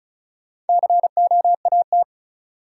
Morse code, heard as a keyed tone, about 36 wpm, 700 Hz. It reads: COAT
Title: Sentences from Top 1600 Words - New Word Drill - 35wpm